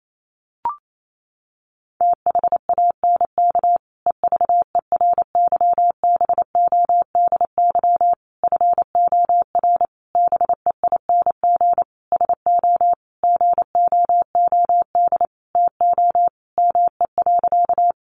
Morse code, heard as a keyed tone, about 28 wpm, 700 Hz. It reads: THANK EVERYBODY FOR BEING SO GOOD TO ME.
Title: Sentences from Top 1400 Words 28wpm